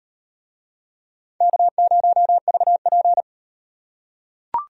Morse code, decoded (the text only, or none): K0VP